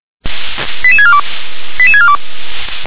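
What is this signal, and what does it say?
Five-tone selective call (ZVEI), heard as quick runs of five tones, tones of about 80 ms. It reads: 88642 88642
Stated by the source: ZVEI five-tone selective call received on SDR.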